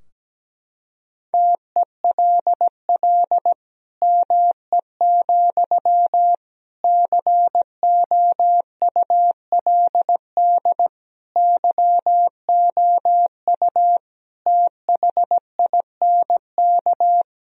Morse code, heard as a keyed tone, about 17 wpm, 700 Hz. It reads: TELL ME, COULD YOU THINK
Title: Sentences from Top 1000 Words - Review Part 2 - 17wpm